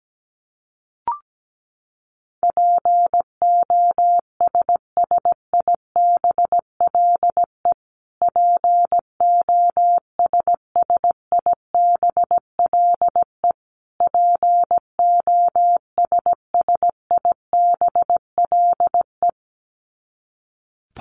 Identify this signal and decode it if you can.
Morse code, heard as a keyed tone, about 17 wpm, 700 Hz. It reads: POSSIBLE POSSIBLE POSSIBLE
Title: Top 500 Words Repeated 17wpm